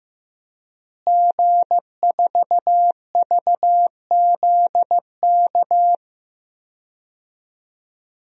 Morse code, heard as a keyed tone, about 15 wpm, 700 Hz. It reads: G4VZK
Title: DXCC-100 Call Signs - 5 Characters - Repeated 2x - 15wpm